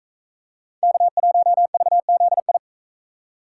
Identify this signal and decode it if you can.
Morse code, heard as a keyed tone, about 43 wpm, 700 Hz. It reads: K1VZI